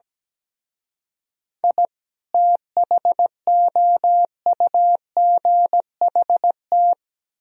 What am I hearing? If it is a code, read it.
Morse code, heard as a keyed tone, about 17 wpm, 700 Hz. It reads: I THOUGHT
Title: Speed-Racing - Sets of 2 Words 17wpm